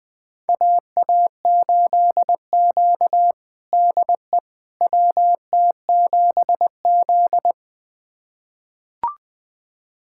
Morse code, heard as a keyed tone, about 20 wpm, 700 Hz. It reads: AA8Q DE WT7Z
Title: USA Callsign DE Callsign - 4 Characters - 20wpm